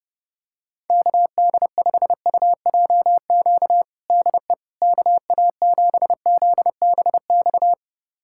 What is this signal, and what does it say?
Morse code, heard as a keyed tone, about 30 wpm, 700 Hz. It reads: KD5UJQ DE KA7ZBX
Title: USA Callsign DE Callsign - 6 Characters - 30wpm